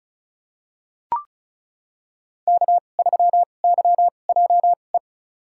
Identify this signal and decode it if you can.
Morse code, heard as a keyed tone, about 36 wpm, spaced as for 25 wpm, 700 Hz. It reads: K3YJE
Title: USA Call Signs - 5 Characters - 2x Character Spacing - 35wpm